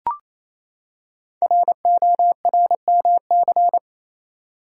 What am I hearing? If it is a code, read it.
Morse code, heard as a keyed tone, about 27 wpm, 700 Hz. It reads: RORMC
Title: Code Groups - 5 Letters - 28wpm